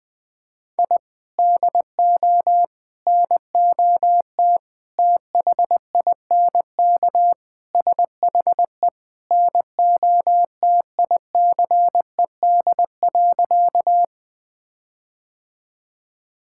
Morse code, heard as a keyed tone, about 20 wpm, 700 Hz. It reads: I DO NOT THINK SHE NOTICED.
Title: Sentences with 5-Word Phrases - Spoken Beginning - 20wpm